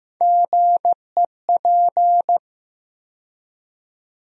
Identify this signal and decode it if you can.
Morse code, heard as a keyed tone, about 15 wpm, 700 Hz. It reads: GEP
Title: USA Call Sign Postfixes - 3 Letters 15wpm